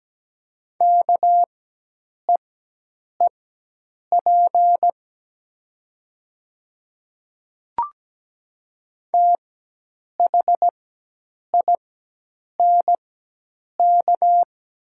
Morse code, heard as a keyed tone, about 17 wpm, spaced as for 8 wpm, 700 Hz. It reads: KEEP THINK
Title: 4x Character Spacing - Top 300 Words - 17wpm